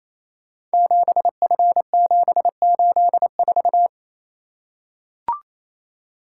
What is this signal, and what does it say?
Morse code, heard as a keyed tone, about 28 wpm, 700 Hz. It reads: 7F784
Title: Code Groups - 5 Letter:Numbers - 28wpm